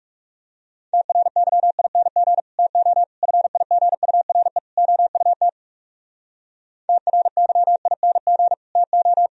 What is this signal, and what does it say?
Morse code, heard as a keyed tone, about 37 wpm, 700 Hz. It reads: TRYING TO FIGURE OUT TRYING TO
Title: Speed-Racing - Sets of 4 Words 30wpm